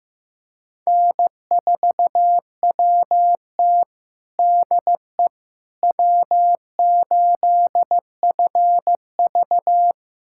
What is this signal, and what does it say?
Morse code, heard as a keyed tone, about 15 wpm, 700 Hz. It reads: N4WT DE W8FV
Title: USA Callsign DE Callsign - 4 Characters - 15wpm